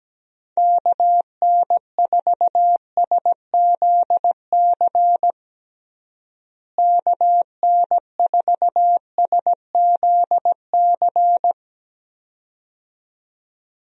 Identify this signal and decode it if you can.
Morse code, heard as a keyed tone, about 17 wpm, 700 Hz. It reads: KN4SZC KN4SZC
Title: USA Call Signs - 6 Characters - Repeated 2x - 17wpm